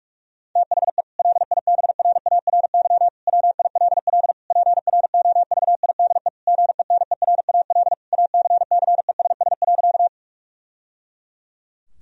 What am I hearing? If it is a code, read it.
Morse code, heard as a keyed tone, about 46 wpm, 700 Hz. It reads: THE LIBRARY WILL PROVIDE GENERAL ACCESS.
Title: Sentences from Top 1000 Words - Review Part 4 - 45wpm